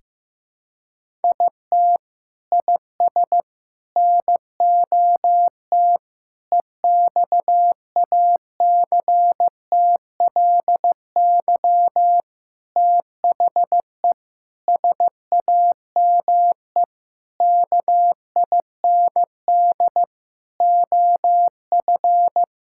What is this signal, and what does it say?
Morse code, heard as a keyed tone, about 16 wpm, 700 Hz. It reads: IT IS NOT EXACTLY THE SAME KIND OF
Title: Sentences from Top 1000 Words - Review Part 6 - 15wpm